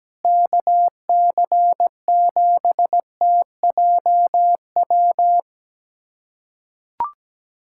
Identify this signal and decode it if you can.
Morse code, heard as a keyed tone, about 17 wpm, 700 Hz. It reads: KC7TJW
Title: USA Callsign DE Callsign - 6 Characters - 17wpm